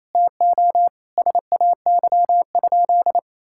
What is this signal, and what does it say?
Morse code, heard as a keyed tone, about 28 wpm, 700 Hz. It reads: TO SAY?
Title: Sentences from Top 1000 Words - Review Part 6 - 28wpm